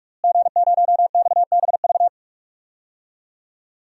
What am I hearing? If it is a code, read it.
Morse code, heard as a keyed tone, about 46 wpm, 700 Hz. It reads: G0XBV
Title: DXCC-100 Call Signs - 5 Characters - Repeated 3x - 45wpm